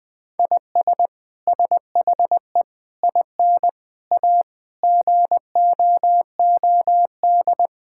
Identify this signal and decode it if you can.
Morse code, heard as a keyed tone, about 20 wpm, 700 Hz. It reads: IS SHE IN A GOOD
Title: Sentences from Top 1000 Words - Review Part 6 - 20wpm